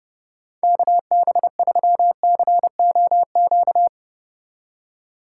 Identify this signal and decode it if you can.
Morse code, heard as a keyed tone, about 29 wpm, 700 Hz. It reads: KB3COQ
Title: USA Call Signs - 6 Characters - 2x Character Spacing - 30wpm